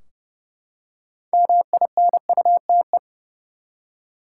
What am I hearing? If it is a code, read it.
Morse code, heard as a keyed tone, about 29 wpm, 700 Hz. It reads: MINUTE
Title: Six Letter Words - 30wpm